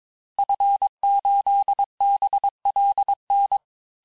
Morse code, heard as a keyed tone, about 22 wpm, 800 Hz. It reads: F8BLN
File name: cours-du-mercredi / audios / cours-854-liste-fin-QSO-vitesse-21-mpm